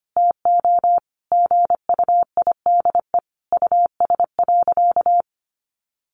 Morse code, heard as a keyed tone, about 25 wpm, 700 Hz. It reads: TO GUIDE US.
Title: Sentences from Top 800 Words 25wpm